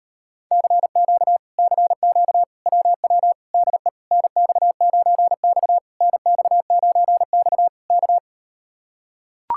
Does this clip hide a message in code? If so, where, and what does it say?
Yes: Morse code from 0.5 to 8.2 s, heard as a keyed tone, about 39 wpm, 700 Hz. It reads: CQ CQ WW DE NX9X NX9X K